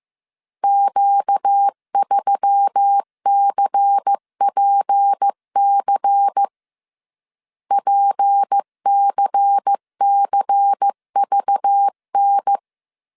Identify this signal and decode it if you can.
Morse code, heard as a keyed tone, about 15 wpm, 800 Hz. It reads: Q3CPC PCCVN
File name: CW / DET / 1 6 1500C S C 14wpm 2951